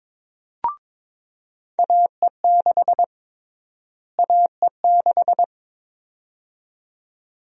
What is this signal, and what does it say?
Morse code, heard as a keyed tone, about 21 wpm, 700 Hz. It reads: AE6 AE6
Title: USA Call Sign Prefixes - Repeated 2x - 22wpm